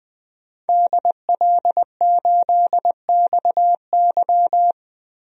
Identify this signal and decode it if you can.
Morse code, heard as a keyed tone, about 20 wpm, 700 Hz. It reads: DL8XY
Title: DXCC-100 Call Signs - 5 Characters - 20wpm